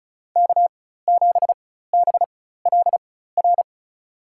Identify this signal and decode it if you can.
Morse code, heard as a keyed tone, about 36 wpm, spaced as for 16 wpm, 700 Hz. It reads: K7BLR
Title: USA Call Signs - 5 Characters - 4x Character Spacing - 35wpm